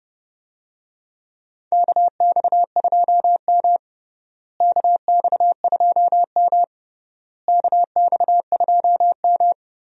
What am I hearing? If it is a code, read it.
Morse code, heard as a keyed tone, about 30 wpm, 700 Hz. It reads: KX2M KX2M KX2M
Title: USA Call Signs - 4 Characters - Repeated 3x - 30wpm